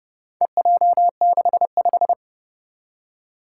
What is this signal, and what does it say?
Morse code, heard as a keyed tone, about 30 wpm, 690 Hz. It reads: EJ65
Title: Four Challenging Letter-Number Combinations 30wpm